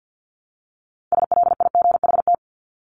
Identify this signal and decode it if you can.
Morse code, heard as a keyed tone, about 48 wpm, 700 Hz. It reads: SLIGHT
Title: LY Suffix - Top 100 Words with Base Words 50wpm